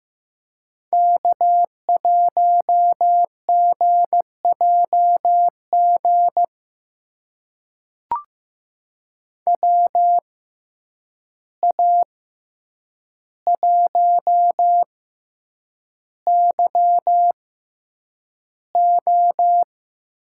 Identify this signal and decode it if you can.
Morse code, heard as a keyed tone, about 15 wpm, spaced as for 8 wpm, 700 Hz. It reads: K1GJG WA1YO
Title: USA Call Signs - 6x Character Spacing - 15wpm